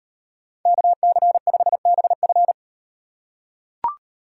Morse code, heard as a keyed tone, about 38 wpm, 700 Hz. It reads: KC5BF